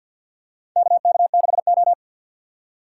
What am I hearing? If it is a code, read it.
Morse code, heard as a keyed tone, about 50 wpm, 690 Hz. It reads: KK6O
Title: USA Call Signs - 4 Characters - 6x Character Spacing - 50wpm